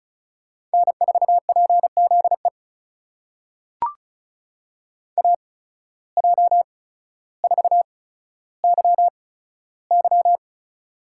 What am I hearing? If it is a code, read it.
Morse code, heard as a keyed tone, about 36 wpm, spaced as for 15 wpm, 700 Hz. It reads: N4PZE AJ4YY
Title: USA Call Signs - 8x Character Spacing - 35wpm